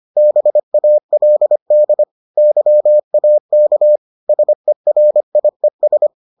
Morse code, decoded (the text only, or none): BALD YAK SERIES